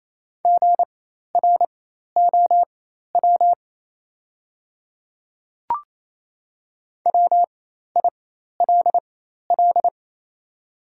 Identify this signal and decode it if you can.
Morse code, heard as a keyed tone, about 28 wpm, spaced as for 13 wpm, 700 Hz. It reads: GROW WILL
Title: 4x Character Spacing - Top 300 Words - 28wpm